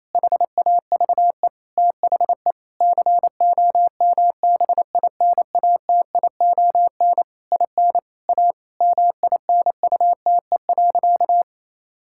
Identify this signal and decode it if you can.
Morse code, heard as a keyed tone, about 27 wpm, 700 Hz. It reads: HAVE THE COMBINATION IN A MINUTE.
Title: Sentences from Top 1000 Words - Review Part 6 - 28wpm